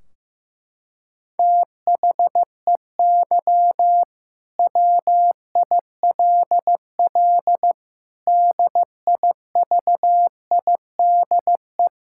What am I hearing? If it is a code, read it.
Morse code, heard as a keyed tone, about 15 wpm, 700 Hz. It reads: THEY WILL DIVIDE
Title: Sentences from Top 1600 Words 15wpm